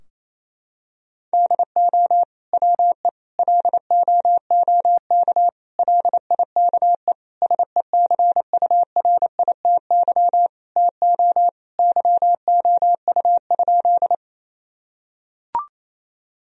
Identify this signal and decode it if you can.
Morse code, heard as a keyed tone, about 28 wpm, 700 Hz. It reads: DO WE LOOK LIKE SECURITY TO YOU?
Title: Sentences from Top 1000 Words - Review Part 4 - 28wpm